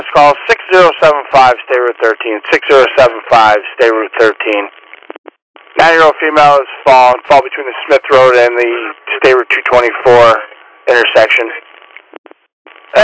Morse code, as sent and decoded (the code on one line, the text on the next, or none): none